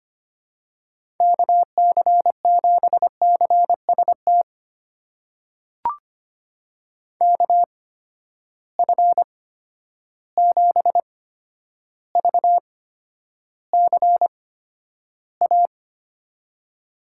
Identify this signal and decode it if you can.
Morse code, heard as a keyed tone, about 25 wpm, spaced as for 11 wpm, 700 Hz. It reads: KC7CST KF7VCA
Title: USA Call Signs - 6 Characters - 8x Character Spacing - 25wpm